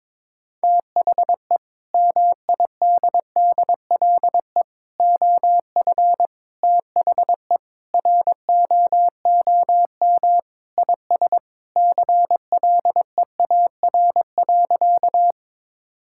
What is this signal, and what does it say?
Morse code, heard as a keyed tone, about 22 wpm, 700 Hz. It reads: THE MIDDLE OF THE ROOM IS CLEAR.
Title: Sentences from Top 1000 Words - Review Part 6 - 22wpm